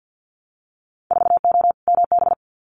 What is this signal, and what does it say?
Morse code, heard as a keyed tone, about 50 wpm, 700 Hz. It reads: 4O MB
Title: Winter Field Day Exchanges Encore 50wpm